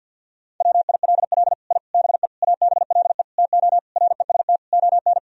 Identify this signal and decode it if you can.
Morse code, heard as a keyed tone, about 50 wpm, 700 Hz. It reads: WILL I BE ABLE TO REST ON
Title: Sentences from Top 1000 Words - Review Part 3 - 50wpm